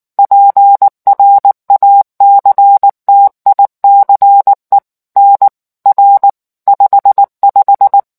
Morse code, heard as a keyed tone, about 19 wpm, 800 Hz. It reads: PRACTICE NR 55